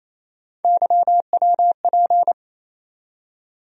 Morse code, heard as a keyed tone, about 28 wpm, 700 Hz. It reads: YWP
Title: USA Call Sign Postfixes - 3 Letters 28wpm